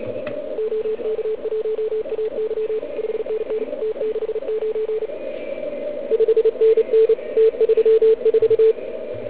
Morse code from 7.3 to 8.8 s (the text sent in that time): T34